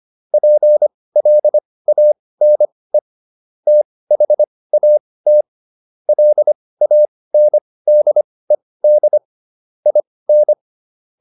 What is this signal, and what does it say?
Morse code, heard as a keyed tone, about 25 wpm, spaced as for 18 wpm, 600 Hz. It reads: PLANE THAT LANDED IN